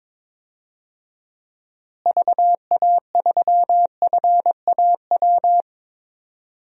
Morse code, heard as a keyed tone, about 22 wpm, 700 Hz. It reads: VA3FAW
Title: DXCC-100 Call Signs - 6 Characters - Repeated 3x - 22wpm